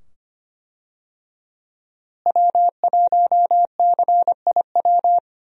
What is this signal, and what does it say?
Morse code, heard as a keyed tone, about 25 wpm, 700 Hz. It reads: W1CIW
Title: USA Call Signs - 5 Characters - Repeated 2x - 25wpm